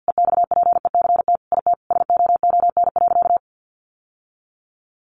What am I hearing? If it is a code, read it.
Morse code, heard as a keyed tone, about 50 wpm, 700 Hz. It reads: EXPECT IT SOON.
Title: Sentences from Top 1100 Words 50wpm